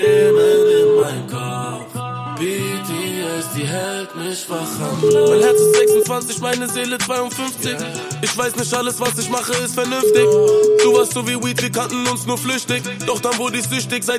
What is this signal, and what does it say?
Ringback tone (Europe, 425 Hz), with noise.